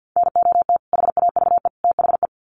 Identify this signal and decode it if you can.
Morse code, heard as a keyed tone, about 50 wpm, 700 Hz. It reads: NOT HAVE THE
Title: Sets of 3 Words 50wpm